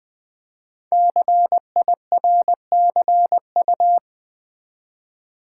Morse code, heard as a keyed tone, about 20 wpm, 700 Hz. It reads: CIRCU
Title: Build A Word - 5 Letters At A Time - 12-Letter Words - 20wpm